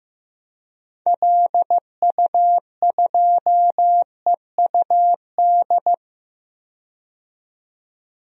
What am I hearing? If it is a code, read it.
Morse code, heard as a keyed tone, about 15 wpm, 700 Hz. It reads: LU2EUD
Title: DXCC-100 Call Signs - 6 Characters - 15wpm